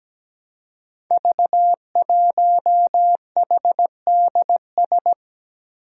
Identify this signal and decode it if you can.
Morse code, heard as a keyed tone, about 17 wpm, 700 Hz. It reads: V1HDS
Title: Five Challenging Letter-Number Combinations 17wpm